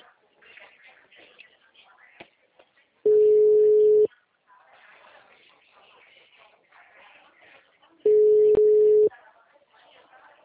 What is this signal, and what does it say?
Ringback tone (Europe, 425 Hz), with speech.
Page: Limeira - Telemarket